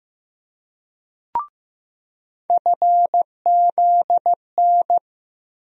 Morse code, heard as a keyed tone, about 15 wpm, 700 Hz. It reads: FZN
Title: USA Call Sign Postfixes - 3 Letters 15wpm